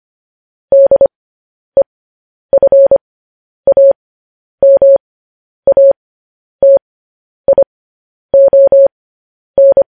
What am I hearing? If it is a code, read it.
Morse code, heard as a keyed tone, about 25 wpm, spaced as for 10 wpm, 550 Hz. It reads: DEFAMATION
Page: news headlines in morse code at 10 wpm